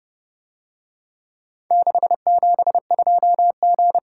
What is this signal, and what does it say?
Morse code, heard as a keyed tone, about 31 wpm, 700 Hz. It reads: 672G